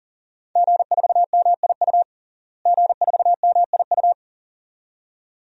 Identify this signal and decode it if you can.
Morse code, heard as a keyed tone, about 37 wpm, 700 Hz. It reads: G4MIU G4MIU
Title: DXCC-100 Call Signs - 5 Characters - Repeated 2x - 40wpm